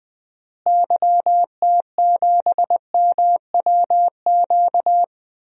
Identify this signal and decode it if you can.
Morse code, heard as a keyed tone, about 21 wpm, 700 Hz. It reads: YT7MWQ
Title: DXCC-100 Call Signs - 6 Characters - Repeated 3x - 20wpm